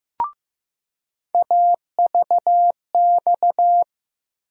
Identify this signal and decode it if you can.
Morse code, heard as a keyed tone, about 15 wpm, 700 Hz. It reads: AVX
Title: USA Call Sign Postfixes - 3 Letters 15wpm